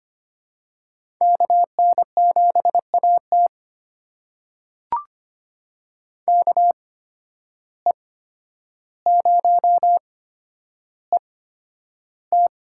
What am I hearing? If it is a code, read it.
Morse code, heard as a keyed tone, about 25 wpm, spaced as for 11 wpm, 700 Hz. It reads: KN7AT KE0ET
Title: USA Call Signs - 8x Character Spacing - 25wpm